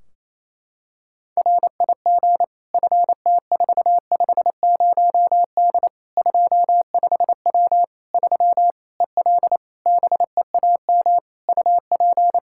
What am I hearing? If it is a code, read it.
Morse code, heard as a keyed tone, about 28 wpm, 700 Hz. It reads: RIG FT450D 25W 3 EL BEAM UP
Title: Practice QSO - Exchange 2 - Info - 28wpm